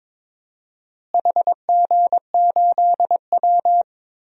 Morse code, heard as a keyed tone, about 22 wpm, 700 Hz. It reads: HG8W
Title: DXCC-100 Call Signs - 4 Characters - Repeated 3x - 22wpm